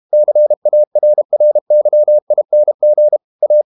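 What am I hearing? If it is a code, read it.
Morse code, heard as a keyed tone, about 32 wpm, 600 Hz. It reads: CARRYING A
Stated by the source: Morse code practice at 32 wpm from CW med Gnister, reading Ken Follett's novel Never in English.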